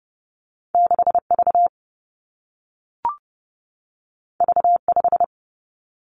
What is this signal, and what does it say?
Morse code, heard as a keyed tone, about 30 wpm, 700 Hz. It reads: 6 V V 5
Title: Two Challenging Letter-Number Combinations 30wpm